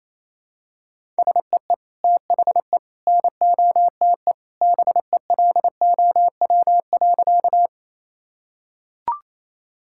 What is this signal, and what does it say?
Morse code, heard as a keyed tone, about 28 wpm, 700 Hz. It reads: SEE THE NOTE BELOW.
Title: Sentences from Top 1000 Words - Review Part 5 - 28wpm